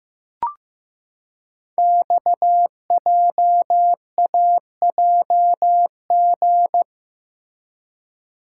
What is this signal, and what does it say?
Morse code, heard as a keyed tone, about 15 wpm, 700 Hz. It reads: XJAJG
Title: Code Groups - 5 Letters - 15wpm